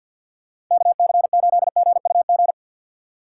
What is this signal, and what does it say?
Morse code, heard as a keyed tone, about 51 wpm, 700 Hz. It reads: KC8GUG
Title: USA Call Signs - 6 Characters - 6x Character Spacing - 50wpm